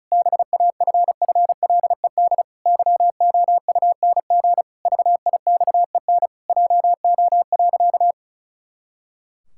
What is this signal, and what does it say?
Morse code, heard as a keyed tone, about 36 wpm, 700 Hz. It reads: BAFFLED YOUNG VIXEN JO.
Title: Pangram Challenge v2 - 35wpm